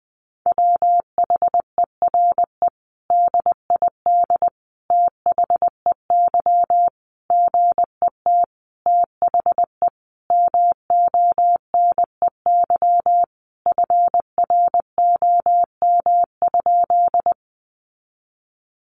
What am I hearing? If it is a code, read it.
Morse code, heard as a keyed tone, about 20 wpm, 700 Hz. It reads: WHERE DID THEY GET THE MONEY FROM?
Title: Sentences from Top 400 Words 20wpm